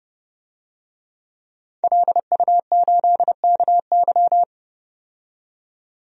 Morse code, heard as a keyed tone, about 30 wpm, 700 Hz. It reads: LU8KY
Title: DXCC-100 Call Signs - 5 Characters - 30wpm